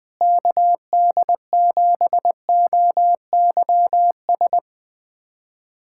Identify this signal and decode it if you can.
Morse code, heard as a keyed tone, about 20 wpm, 700 Hz. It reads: KD7OYS
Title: USA Callsign DE Callsign - 6 Characters - 20wpm